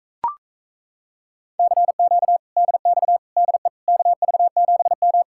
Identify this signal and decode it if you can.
Morse code, heard as a keyed tone, about 37 wpm, 700 Hz. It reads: CQ DX DE KV7M